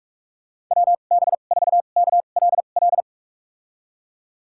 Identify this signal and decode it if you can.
Morse code, heard as a keyed tone, about 46 wpm, spaced as for 32 wpm, 700 Hz. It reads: WB4KLL
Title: USA Call Signs - 6 Characters - 2x Character Spacing - 45wpm